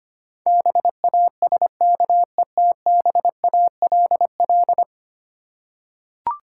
Morse code, heard as a keyed tone, about 25 wpm, 700 Hz. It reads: BASKETBALL
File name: Ten Letter Words - 25wpm